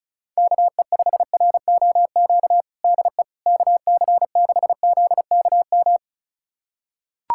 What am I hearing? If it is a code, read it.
Morse code, heard as a keyed tone, about 36 wpm, 700 Hz. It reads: KE5ROQ DE KC6ZKM
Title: USA Callsign DE Callsign - 6 Characters - 35wpm